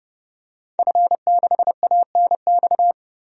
Morse code, heard as a keyed tone, about 30 wpm, 700 Hz. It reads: F6ANX
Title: DXCC-100 Call Signs - 5 Characters - 30wpm